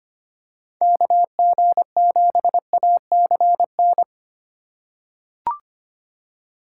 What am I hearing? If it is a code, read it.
Morse code, heard as a keyed tone, about 25 wpm, 700 Hz. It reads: KG7ACN
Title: USA Call Signs - 6 Characters - 2x Character Spacing - 25wpm